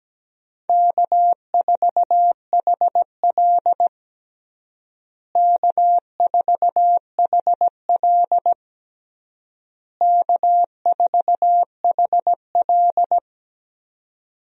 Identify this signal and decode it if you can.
Morse code, heard as a keyed tone, about 17 wpm, 700 Hz. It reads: K4HL K4HL K4HL
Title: USA Call Signs - 4 Characters - Repeated 3x - 17wpm